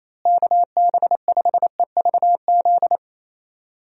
Morse code, heard as a keyed tone, about 28 wpm, 700 Hz. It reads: KB5EVZ
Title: USA Call Signs - 6 Characters - Repeated 2x - 28wpm